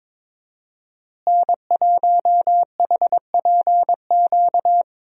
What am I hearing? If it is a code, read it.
Morse code, heard as a keyed tone, about 22 wpm, 700 Hz. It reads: N1HPQ
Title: USA Call Signs - 5 Characters - Repeated 2x - 22wpm